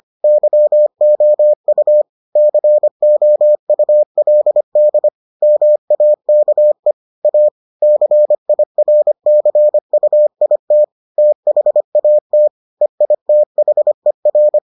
Morse code, heard as a keyed tone, about 25 wpm, 600 Hz. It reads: YOU COULD MAKE A CIRCUIT THAT EITHER